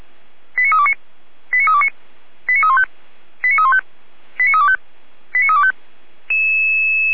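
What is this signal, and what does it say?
Five-tone selective call (ZVEI), heard as quick runs of five tones, tones of about 70 ms. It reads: 89329 89329 89315 89315 89325 89325
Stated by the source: ZVEI five-tone call received on SDR.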